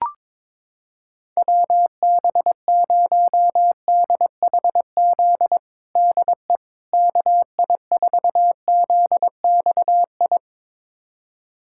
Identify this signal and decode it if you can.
Morse code, heard as a keyed tone, about 21 wpm, 700 Hz. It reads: WB0DHZ DE KI4ZXI
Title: USA Callsign DE Callsign - 6 Characters - 22wpm